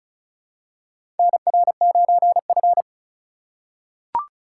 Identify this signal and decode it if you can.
Morse code, heard as a keyed tone, about 34 wpm, 700 Hz. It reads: NR9F
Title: USA Call Signs - 4 Characters - 3x Character Spacing - 35wpm